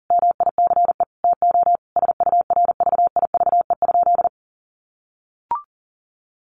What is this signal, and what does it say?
Morse code, heard as a keyed tone, about 40 wpm, 700 Hz. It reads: MICE TO SURVIVE?
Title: A Sound Of Thunder 40wpm